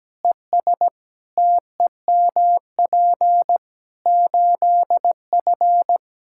Morse code, heard as a keyed tone, about 17 wpm, 700 Hz. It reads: ES TEMP 8F
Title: Practice QSO - Exchange 2 - Info - 17wpm